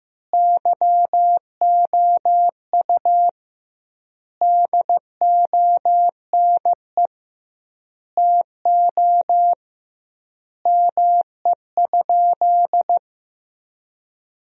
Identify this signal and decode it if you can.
Morse code, heard as a keyed tone, about 15 wpm, 700 Hz. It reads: YOU DONE TO ME?
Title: 2x Word-Spacing - Sentences from Top 500 Words - 15wpm